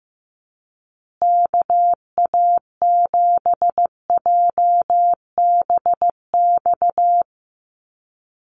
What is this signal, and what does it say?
Morse code, heard as a keyed tone, about 15 wpm, 700 Hz. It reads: KA7JBX
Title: USA Call Signs Repeated 3x 15wpm